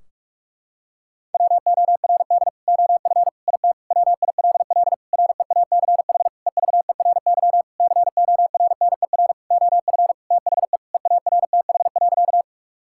Morse code, heard as a keyed tone, about 41 wpm, 700 Hz. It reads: WORD OF IT WILL REACH EVERY CORNER OF THE EARTH.
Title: Sentences from Top 1600 Words 45wpm